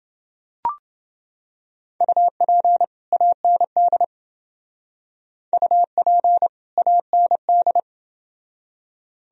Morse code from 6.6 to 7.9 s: AND

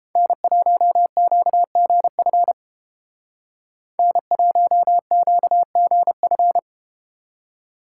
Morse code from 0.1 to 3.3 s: N1QGF

